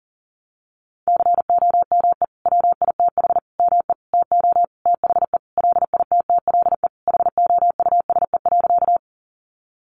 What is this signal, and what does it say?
Morse code, heard as a keyed tone, about 40 wpm, 700 Hz. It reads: COME WITH ME TO THE LITTLE HOUSE.
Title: Sentences from Top 200 Words 40wpm